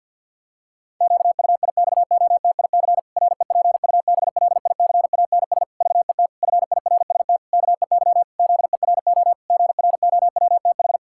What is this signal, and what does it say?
Morse code, heard as a keyed tone, about 47 wpm, 700 Hz. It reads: QUIXOTIC REPUBLICANS VET FIRST KEY ZERO GROWTH
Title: Pangram Challenge v2 - 50wpm